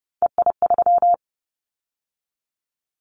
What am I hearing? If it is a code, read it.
Morse code, heard as a keyed tone, about 30 wpm, 700 Hz. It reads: EI3
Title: Callsign Prefixes for Top 50 DXCC - 30wpm